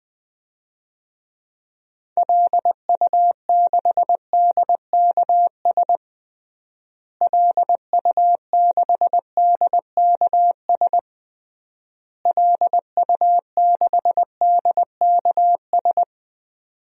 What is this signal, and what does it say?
Morse code, heard as a keyed tone, about 21 wpm, 700 Hz. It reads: LU6DKS LU6DKS LU6DKS
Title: DXCC-100 Call Signs - 6 Characters - Repeated 3x - 20wpm